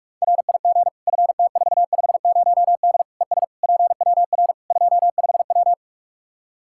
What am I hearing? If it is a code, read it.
Morse code, heard as a keyed tone, about 41 wpm, 700 Hz. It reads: RIG FT450D ES PWR 25W
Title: K1USN SST QSO Practice 45wpm